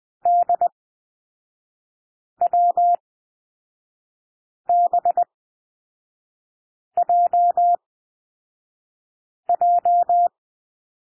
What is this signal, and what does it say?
Morse code, heard as a keyed tone, about 20 wpm, spaced as for 5 wpm, 700 Hz. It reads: DWBJJ
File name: SOUND LETTERS 20 wpm